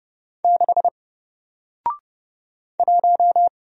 Morse code, heard as a keyed tone, about 29 wpm, 700 Hz. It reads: 6 1
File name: Single-Letter-Number-30wpm